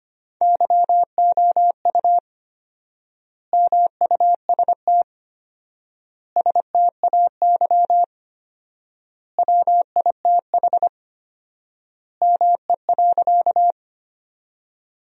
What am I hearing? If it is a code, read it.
Morse code, heard as a keyed tone, about 25 wpm, 700 Hz. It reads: YOU MUST STAY WITH ME.
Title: 4x Word-Spacing - Sentences from Top 500 Words - 25wpm